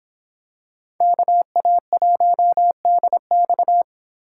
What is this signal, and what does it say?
Morse code, heard as a keyed tone, about 24 wpm, 700 Hz. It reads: KA1DX